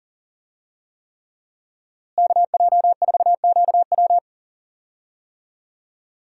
Morse code, heard as a keyed tone, about 40 wpm, 700 Hz. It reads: KJ4QW
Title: USA Call Signs - 5 Characters - Repeated 3x - 40wpm